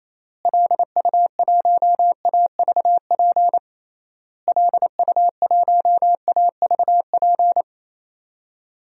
Morse code, heard as a keyed tone, about 28 wpm, 700 Hz. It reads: LU1AVP LU1AVP
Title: DXCC-100 Call Signs - 6 Characters - Repeated 3x - 28wpm